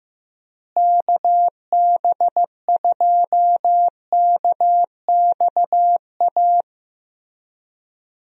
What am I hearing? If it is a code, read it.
Morse code, heard as a keyed tone, about 15 wpm, 700 Hz. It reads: KB2KXA
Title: USA Call Signs - 6 Characters - Repeated 3x - 15wpm